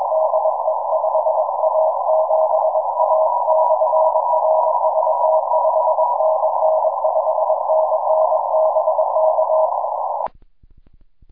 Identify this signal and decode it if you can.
Morse code, heard as a keyed tone, about 21 wpm, 630 Hz, with noise. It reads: DE 5A7A /RST 5NN 5NN BK
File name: iw0ffk / 2006 11 16 5A7A IW0FFK